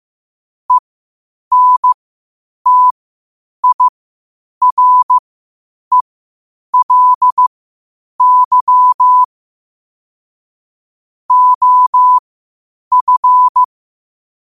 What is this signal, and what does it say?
Morse code, heard as a keyed tone, about 15 wpm, spaced as for 8 wpm, 1000 Hz. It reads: ENTIRELY OF